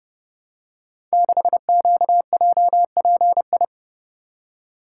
Morse code, heard as a keyed tone, about 30 wpm, 700 Hz. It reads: 6QJPI
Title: Code Groups - 5 Letter:Numbers - 30wpm